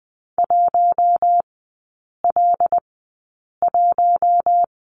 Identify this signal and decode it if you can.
Morse code, heard as a keyed tone, about 20 wpm, 700 Hz. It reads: 1 L 1
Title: 114 Lesson 19e - Letter w - Three Characters 20wpm